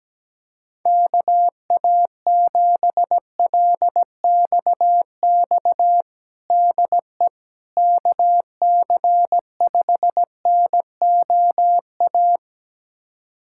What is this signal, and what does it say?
Morse code, heard as a keyed tone, about 17 wpm, 700 Hz. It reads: KA7LXX DE KC5NOA
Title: USA Callsign DE Callsign 17wpm